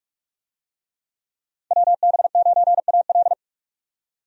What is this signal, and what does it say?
Morse code, heard as a keyed tone, about 43 wpm, 700 Hz. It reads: WB9AL
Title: USA Call Signs - 5 Characters - 45wpm